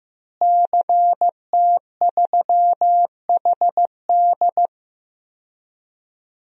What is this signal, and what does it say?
Morse code, heard as a keyed tone, about 15 wpm, 700 Hz. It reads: CT3HD
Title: DXCC-100 Call Signs - 5 Characters - Repeated 2x - 15wpm